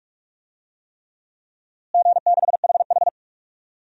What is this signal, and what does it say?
Morse code, heard as a keyed tone, about 42 wpm, 700 Hz. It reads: G6HH
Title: DXCC-100 Call Signs - 4 Characters - Repeated 3x - 45wpm